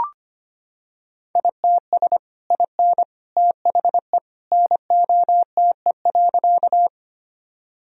Morse code, heard as a keyed tone, about 25 wpm, 700 Hz. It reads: ITS IN THE NOTE.
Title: Sentences from Top 1000 Words - Review Part 2 - 25wpm